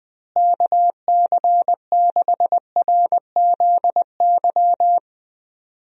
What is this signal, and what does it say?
Morse code, heard as a keyed tone, about 20 wpm, 700 Hz. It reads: KC6RZY
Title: USA Call Signs - 6 Characters - Repeated 3x - 20wpm